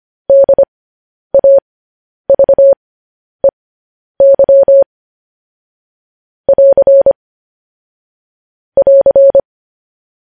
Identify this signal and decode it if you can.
Morse code, heard as a keyed tone, about 25 wpm, spaced as for 10 wpm, 550 Hz. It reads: DAVEY <AR> <AR>
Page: news-headlines-in-morse-code-at-10-wpm